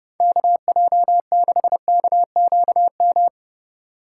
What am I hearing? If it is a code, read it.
Morse code, heard as a keyed tone, about 30 wpm, 700 Hz. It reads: KJ6KQM
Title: USA Call Signs - 6 Characters - 6x Character Spacing - 30wpm